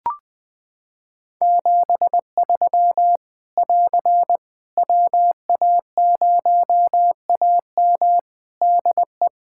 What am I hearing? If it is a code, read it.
Morse code, heard as a keyed tone, about 21 wpm, 700 Hz. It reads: 73 <AR> WA0AM DE
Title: K1USN SST QSO Practice 20wpm